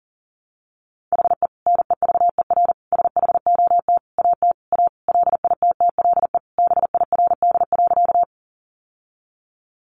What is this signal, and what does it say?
Morse code, heard as a keyed tone, about 40 wpm, 700 Hz. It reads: HE NEVER SHOT AT A LITTLE BIRD.
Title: Sentences from Top 1100 Words 40wpm